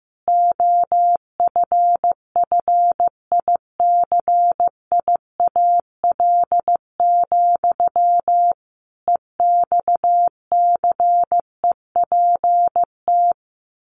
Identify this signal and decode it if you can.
Morse code, heard as a keyed tone, about 15 wpm, 700 Hz. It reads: OFFICIAL, EXCEPT
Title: A Sound Of Thunder 15wpm